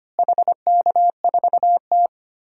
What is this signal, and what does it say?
Morse code, heard as a keyed tone, about 25 wpm, 700 Hz. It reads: HK4T
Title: DXCC-100 Call Signs - 4 Characters - Repeated 2x - 25wpm